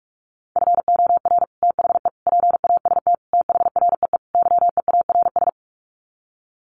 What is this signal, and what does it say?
Morse code, heard as a keyed tone, about 44 wpm, 700 Hz. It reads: FOR THE PAST THREE YEARS
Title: Sets of 5 Words Encore2 45wpm